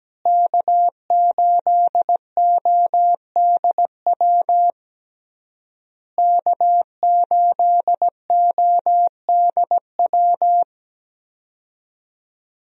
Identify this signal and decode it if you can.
Morse code, heard as a keyed tone, about 17 wpm, 700 Hz. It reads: K8ODW K8ODW
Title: USA Call Signs - 5 Characters - Repeated 3x - 17wpm